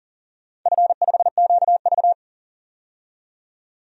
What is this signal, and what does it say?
Morse code, heard as a keyed tone, about 40 wpm, 700 Hz. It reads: F5QV
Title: DXCC-100 Call Signs - 4 Characters - 40wpm